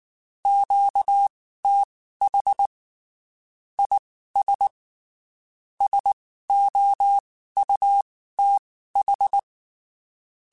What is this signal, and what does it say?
Morse code, heard as a keyed tone, about 19 wpm, spaced as for 14 wpm, 800 Hz. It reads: QTH IS SOUTH
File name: (spaced) / QSOs Spaced-mp3s / WA7QOX 2 @18 sp